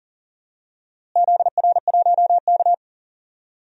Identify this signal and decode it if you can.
Morse code, heard as a keyed tone, about 40 wpm, 700 Hz. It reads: ZR1K
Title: DXCC-100 Call Signs - 4 Characters - Repeated 2x - 40wpm